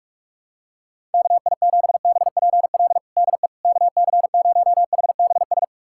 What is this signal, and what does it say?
Morse code, heard as a keyed tone, about 44 wpm, 700 Hz. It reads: KI7BPL DE KC0HBS
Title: USA Callsign DE Callsign - 6 Characters - 45wpm